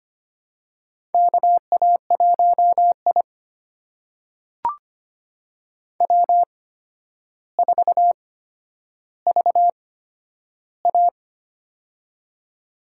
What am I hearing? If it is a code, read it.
Morse code, heard as a keyed tone, about 25 wpm, spaced as for 11 wpm, 700 Hz. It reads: KA1I W4VA
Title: USA Call Signs - 4 Characters - 8x Character Spacing - 25wpm